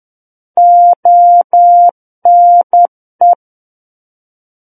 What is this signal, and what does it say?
Morse code, heard as a keyed tone, about 10 wpm, 700 Hz. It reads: ONE
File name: Book One / 01-Chapter 1 - 10 WPM